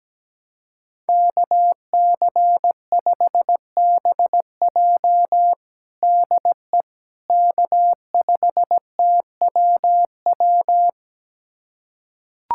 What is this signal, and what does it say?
Morse code, heard as a keyed tone, about 17 wpm, 700 Hz. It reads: KC5BJ DE K5TWW
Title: USA Callsign DE Callsign - 5 Characters - 17wpm